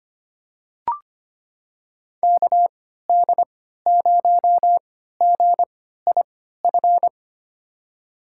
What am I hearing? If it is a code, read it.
Morse code, heard as a keyed tone, about 25 wpm, spaced as for 14 wpm, 700 Hz. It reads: KD0GIF
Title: USA Call Signs - 6 Characters - 3x Character Spacing - 25wpm